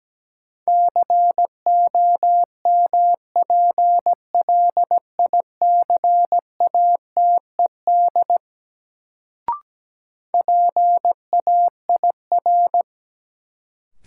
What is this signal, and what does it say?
Morse code, heard as a keyed tone, about 17 wpm, 700 Hz. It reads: COMPLICATED PAIR
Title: Sentences from Top 1200 Words - New Word Drill Alt - 17wpm